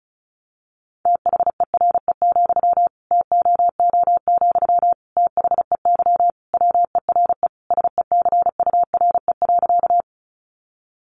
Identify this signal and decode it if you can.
Morse code, heard as a keyed tone, about 34 wpm, 700 Hz. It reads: THERE, TOO, THEY WERE SECURE.
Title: Sentences from Top 1200 Words 35wpm